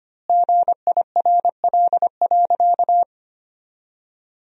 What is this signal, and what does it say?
Morse code, heard as a keyed tone, about 25 wpm, 700 Hz. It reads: GIRL.
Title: 4x Word-Spacing - Sentences from Top 400 Words - 25wpm